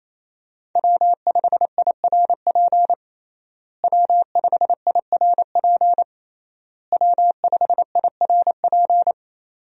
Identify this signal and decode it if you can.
Morse code, heard as a keyed tone, about 28 wpm, 700 Hz. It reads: W5IRP W5IRP W5IRP
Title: USA Call Signs - 5 Characters - Repeated 3x - 28wpm